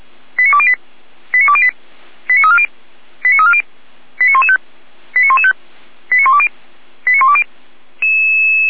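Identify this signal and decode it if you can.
Five-tone selective call (ZVEI), heard as quick runs of five tones, tones of about 70 ms. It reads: 89208 89208 89340 89340 89115 89115 89120 89120
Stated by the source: ZVEI five-tone call received on SDR, 455 channel.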